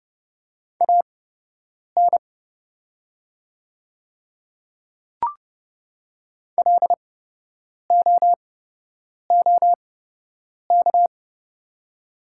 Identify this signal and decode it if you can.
Morse code, heard as a keyed tone, about 30 wpm, spaced as for 8 wpm, 700 Hz. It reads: AN LOOK
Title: 8x Character Spacing - Top 100 Words - 30wpm